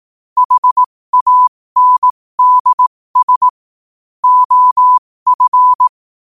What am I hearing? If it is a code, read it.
Morse code, heard as a keyed tone, about 18 wpm, spaced as for 15 wpm, 1000 Hz. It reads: HANDS OF
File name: qotdfiles / 15wpm